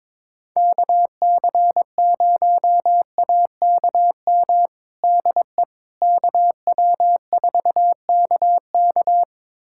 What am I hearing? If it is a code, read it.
Morse code, heard as a keyed tone, about 22 wpm, 700 Hz. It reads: KC0AKM DE KW4KK
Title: USA Callsign DE Callsign 22wpm